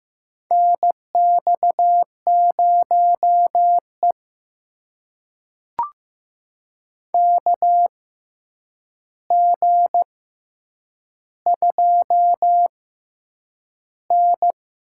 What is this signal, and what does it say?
Morse code, heard as a keyed tone, about 15 wpm, spaced as for 8 wpm, 700 Hz. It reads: NX0E KG2N
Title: USA Call Signs - 4 Characters - 6x Character Spacing - 15wpm